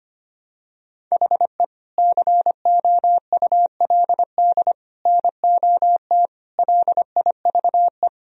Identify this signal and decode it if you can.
Morse code, heard as a keyed tone, about 25 wpm, 700 Hz. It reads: HE COULD NOT LIVE
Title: Sentences from Top 1000 Words - Review Part 3 - 25wpm